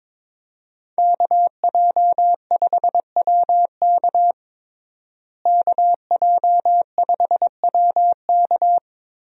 Morse code, heard as a keyed tone, about 22 wpm, 700 Hz. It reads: KJ5WK KJ5WK
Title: USA Call Signs - 5 Characters - Repeated 2x - 22wpm